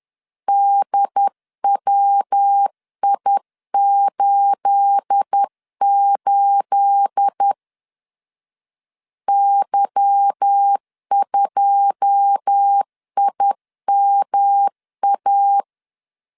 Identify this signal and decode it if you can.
Morse code, heard as a keyed tone, about 10 wpm, 800 Hz. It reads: DWI88 Y2IMA
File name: LEZIONI / 9 28 10wpm 2842